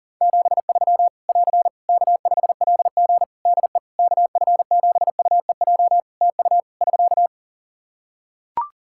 Morse code, heard as a keyed tone, about 40 wpm, 700 Hz. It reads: 73 <AR> K5LG DE KF7UEJ TU <SK>
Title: K1USN SST QSO Practice 40wpm